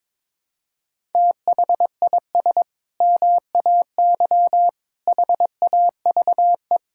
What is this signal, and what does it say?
Morse code, heard as a keyed tone, about 22 wpm, 700 Hz. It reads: THIS MAY HAVE
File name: practice / 22wpm / Sentences with 8-Word Phrases - 22wpm